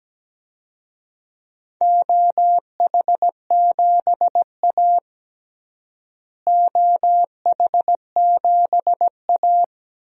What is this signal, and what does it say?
Morse code, heard as a keyed tone, about 17 wpm, 700 Hz. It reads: OH7A OH7A
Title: DXCC-100 Call Signs - 4 Characters - Repeated 2x - 17wpm